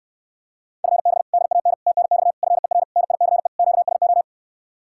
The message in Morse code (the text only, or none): FB DAN TNX FER NICE QSO